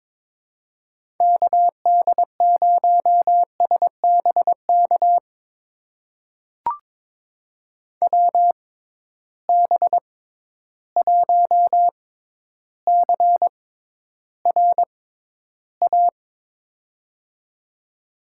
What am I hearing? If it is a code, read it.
Morse code, heard as a keyed tone, about 21 wpm, spaced as for 11 wpm, 700 Hz. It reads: KD0SBK WB1CRA
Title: USA Call Signs - 6 Characters - 6x Character Spacing - 22wpm